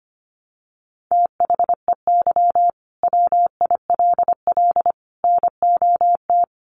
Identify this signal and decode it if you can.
Morse code, heard as a keyed tone, about 25 wpm, 690 Hz. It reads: THEY WILL NOT
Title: Sentences from Top 1100 Words 25wpm